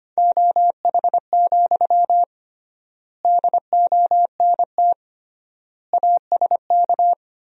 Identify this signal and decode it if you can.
Morse code, heard as a keyed tone, about 25 wpm, 700 Hz. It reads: OH, DONT ASK
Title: 3x Word-Spacing - Sentences from Top 500 Words - 25wpm